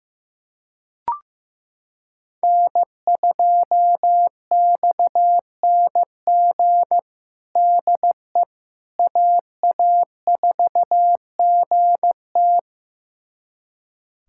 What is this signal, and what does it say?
Morse code, heard as a keyed tone, about 15 wpm, 700 Hz. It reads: N2XNG DE AA4GT
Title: USA Callsign DE Callsign - 5 Characters - 15wpm